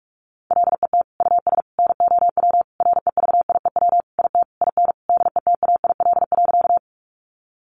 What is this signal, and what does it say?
Morse code, heard as a keyed tone, about 44 wpm, 700 Hz. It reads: LET US NOW REVIEW IT IN DETAIL.
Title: Sentences from Top 1500 Words 45wpm